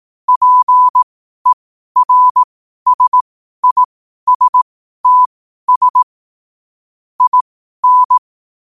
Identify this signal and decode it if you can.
Morse code, heard as a keyed tone, about 18 wpm, spaced as for 12 wpm, 1000 Hz. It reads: PERSISTS IN
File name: qotdfiles / 12wpm